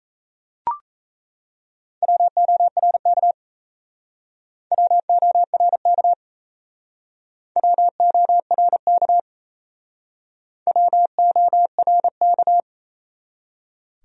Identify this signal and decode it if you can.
Morse code, heard as a keyed tone, about 37 wpm, 700 Hz. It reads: WORK WORK WORK WORK